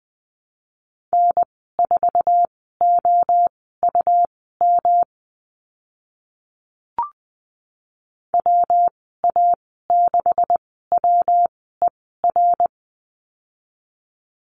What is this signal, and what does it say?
Morse code, heard as a keyed tone, about 20 wpm, spaced as for 14 wpm, 700 Hz. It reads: N4OUM WA6WER
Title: 140 Lesson 23g - Letter f - Call Signs 20wpm